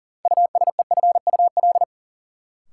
Morse code, heard as a keyed tone, about 39 wpm, 700 Hz. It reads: USEFUL
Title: Sentences from Top 700 Words - New Word Drill - 40wpm